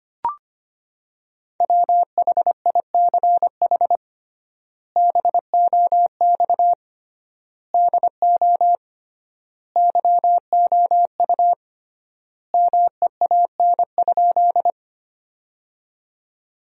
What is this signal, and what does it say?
Morse code, heard as a keyed tone, about 25 wpm, 700 Hz. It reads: WHICH BOX DO YOU MEAN?
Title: 3x Word-Spacing - Sentences from Top 500 Words - 25wpm